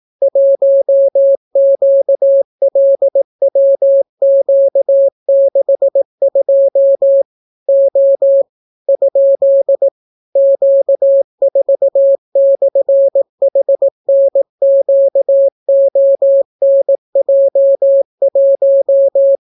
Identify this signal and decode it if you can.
Morse code, heard as a keyed tone, about 18 wpm, 550 Hz. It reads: 1QLWQ62 O ? Q4/HNQONJ1